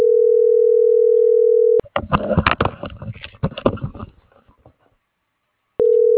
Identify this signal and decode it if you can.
Ringback tone (North America), 440 + 480 Hz.